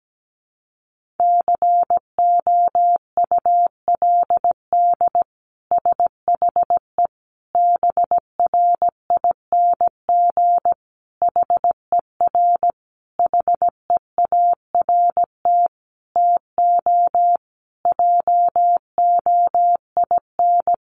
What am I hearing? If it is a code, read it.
Morse code, heard as a keyed tone, about 17 wpm, 700 Hz. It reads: COULD SHE BRING HER HEART TO JOIN
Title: Sentences from Top 1400 Words 17wpm